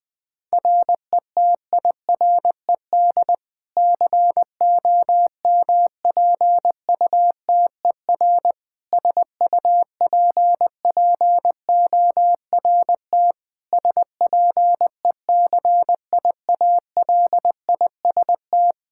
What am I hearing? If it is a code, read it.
Morse code, heard as a keyed tone, about 20 wpm, 700 Hz. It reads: RETIRED COMPUTER SUPPORT SPECIALIST
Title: Practice QSO - Exchange 3 - More Info - 20wpm